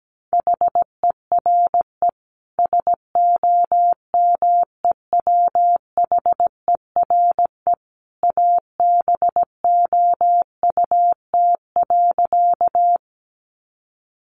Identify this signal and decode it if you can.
Morse code, heard as a keyed tone, about 17 wpm, 700 Hz. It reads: HERE SOMEWHERE ABOUT.
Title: Sentences from Top 1400 Words 17wpm